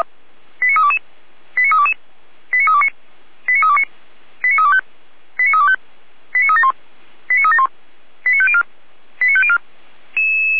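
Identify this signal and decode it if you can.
Five-tone selective call (ZVEI), heard as quick runs of five tones, tones of about 70 ms. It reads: 89322 89322 89329 89329 89325 89325 89371 89371 89504 89504
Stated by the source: ZVEI five-tone selective call 89029 received on SDR.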